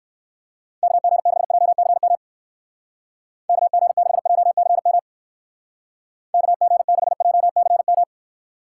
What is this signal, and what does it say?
Morse code, heard as a keyed tone, about 58 wpm, 700 Hz. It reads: KG6JCM KG6JCM KG6JCM